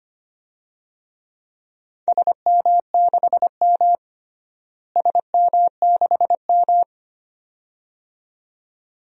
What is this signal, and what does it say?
Morse code, heard as a keyed tone, about 25 wpm, 700 Hz. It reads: SM6M SM6M
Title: DXCC-100 Call Signs - 4 Characters - Repeated 2x - 25wpm